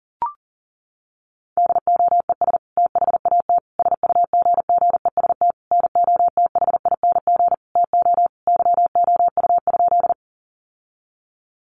Morse code, heard as a keyed tone, about 40 wpm, 700 Hz. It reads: DOES THAT SUGGEST NOTHING TO YOU?
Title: Sentences from Top 1300 Words 40wpm